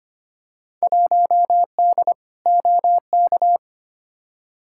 Morse code, heard as a keyed tone, about 25 wpm, 700 Hz. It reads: 1D OK
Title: Field Day Exchanges - 25wpm